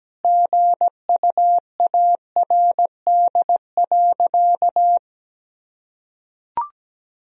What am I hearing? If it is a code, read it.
Morse code, heard as a keyed tone, about 17 wpm, 700 Hz. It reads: GUARD.
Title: Sentences from Top 1000 Words - Review Part 4 - 17wpm